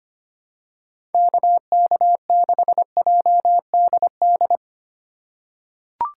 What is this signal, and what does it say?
Morse code, heard as a keyed tone, about 25 wpm, 700 Hz. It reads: KK6JDD
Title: USA Call Signs - 6 Characters - 2x Character Spacing - 25wpm